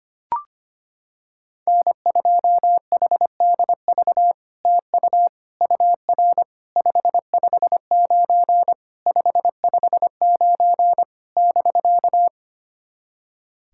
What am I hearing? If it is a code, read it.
Morse code, heard as a keyed tone, about 25 wpm, 700 Hz. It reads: N2HDV TU UR 559 559 <BK>
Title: SOTA QSO Practice 25wpm